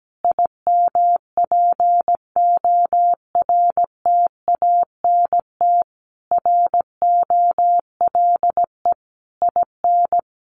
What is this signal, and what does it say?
Morse code, heard as a keyed tone, about 17 wpm, 700 Hz. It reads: IMPORTANT ROLE IN
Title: Sets of 5 Words Encore 17wpm